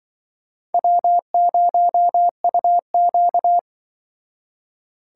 Morse code, heard as a keyed tone, about 24 wpm, 700 Hz. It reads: W0UQ
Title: Speed Racing - USA 1x2 Call Signs 20wpm